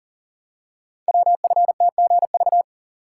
Morse code, heard as a keyed tone, about 41 wpm, 700 Hz. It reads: WFTGV
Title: Code Groups - 5 Letters - 40wpm